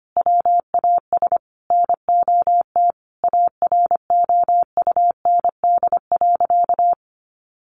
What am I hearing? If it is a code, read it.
Morse code, heard as a keyed tone, about 25 wpm, 700 Hz. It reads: WAS NOT AROUND.
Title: Sentences from Top 1300 Words 25wpm